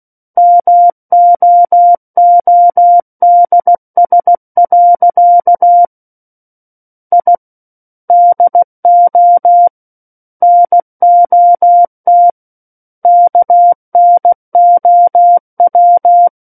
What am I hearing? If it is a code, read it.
Morse code, heard as a keyed tone, about 16 wpm, 700 Hz. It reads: MOODS. I DO NOT KNOW
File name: Book One / 07-Chapter 7 - 16 WPM